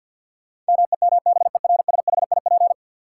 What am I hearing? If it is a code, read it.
Morse code, heard as a keyed tone, about 53 wpm, 700 Hz. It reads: MEMBERSHIP
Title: Sentences from Top 1400 Words - New Word Drill Alt - 50wpm